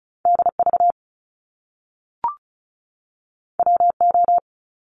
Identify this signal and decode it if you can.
Morse code, heard as a keyed tone, about 36 wpm, 700 Hz. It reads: DV WO
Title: Top Two Letter Combinations in Words 35wpm